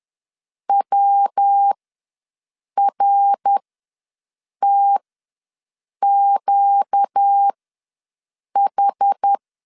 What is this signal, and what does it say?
Morse code, heard as a keyed tone, about 10 wpm, spaced as for 6 wpm, 800 Hz. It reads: WRTQH
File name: LEZIONI / 5 LBVC / 5 L ws 10wpm 3636